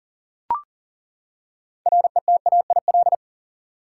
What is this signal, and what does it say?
Morse code, heard as a keyed tone, about 40 wpm, 700 Hz. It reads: RETAIL
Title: Practice QSO - Exchange 3 - More Info - 40wpm